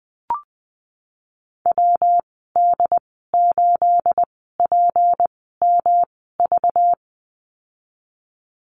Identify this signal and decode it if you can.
Morse code, heard as a keyed tone, about 20 wpm, spaced as for 14 wpm, 700 Hz. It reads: WD8PMV
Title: 219 Lesson 35g - Number 0 - Call Signs 20wpm